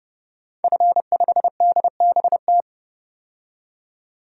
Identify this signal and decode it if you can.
Morse code, heard as a keyed tone, about 29 wpm, 690 Hz. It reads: F5DBT
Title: DXCC-100 Call Signs - 5 Characters - 30wpm